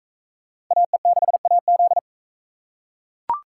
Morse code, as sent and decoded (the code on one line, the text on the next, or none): .- . -.... .- --..
AE6AZ